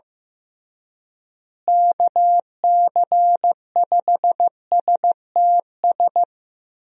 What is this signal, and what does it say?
Morse code, heard as a keyed tone, about 15 wpm, 700 Hz. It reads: KC5STS
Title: USA Call Signs - 6 Characters - Repeated 3x - 15wpm